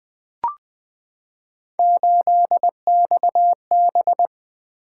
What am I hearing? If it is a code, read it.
Morse code, heard as a keyed tone, about 20 wpm, 700 Hz. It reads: 8XB
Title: Three Letter-Number Combinations - 20wpm